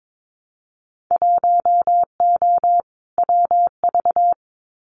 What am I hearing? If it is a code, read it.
Morse code, heard as a keyed tone, about 22 wpm, 700 Hz. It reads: 1O WV
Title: Winter Field Day Exchanges 22wpm